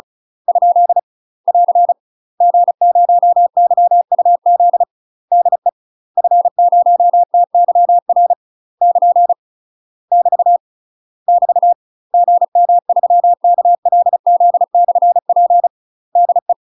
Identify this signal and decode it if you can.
Morse code, heard as a keyed tone, about 35 wpm, 700 Hz. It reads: ? <AR> G0YUZ DE F0TYR <KN> <BT> <BT> GM3KLZ/P DE
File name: UK MASTER QSOs TRAINING / 35WPM / UK MASTER QSOs 0026